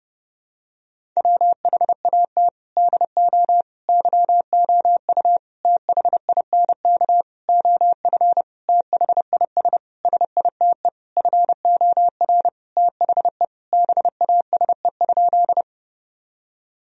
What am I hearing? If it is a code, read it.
Morse code, heard as a keyed tone, about 30 wpm, 700 Hz. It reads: WHAT DO YOU THINK OF THIS SITE FOR THE BASE?
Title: Sentences from Top 1000 Words - Review Part 1 - 30wpm